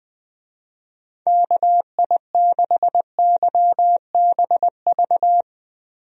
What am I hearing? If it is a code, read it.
Morse code, heard as a keyed tone, about 20 wpm, 700 Hz. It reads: KI6YBV
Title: USA Call Signs - 6 Characters - Repeated 3x - 20wpm